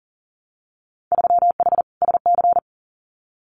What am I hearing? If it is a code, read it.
Morse code, heard as a keyed tone, about 41 wpm, 700 Hz. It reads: 3H SC
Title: Winter Field Day Exchanges 40wpm